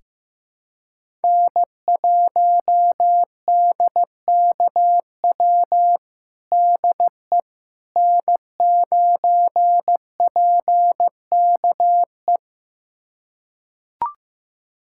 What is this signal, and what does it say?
Morse code, heard as a keyed tone, about 15 wpm, 700 Hz. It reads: N1DKW DE N9PKE
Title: USA Callsign DE Callsign - 5 Characters - 15wpm